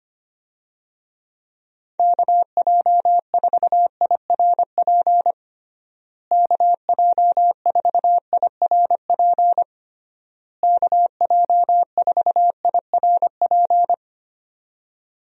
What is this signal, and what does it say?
Morse code, heard as a keyed tone, about 25 wpm, 700 Hz. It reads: KJ4IRP KJ4IRP KJ4IRP
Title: USA Call Signs - 6 Characters - Repeated 3x - 25wpm